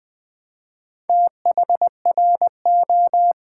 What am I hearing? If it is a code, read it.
Morse code, heard as a keyed tone, about 20 wpm, 700 Hz. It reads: THRO
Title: Build A Word - 4 Letters At A Time - 10-Letter Words - 20wpm